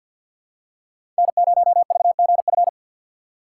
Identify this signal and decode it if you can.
Morse code, heard as a keyed tone, about 46 wpm, 700 Hz. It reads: N0VGF
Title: USA Call Signs - 4x Character Spacing - 50wpm